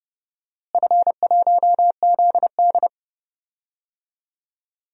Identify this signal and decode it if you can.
Morse code, heard as a keyed tone, about 30 wpm, 700 Hz. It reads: F1ZD
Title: DXCC-100 Call Signs - 4 Characters - Repeated 2x - 30wpm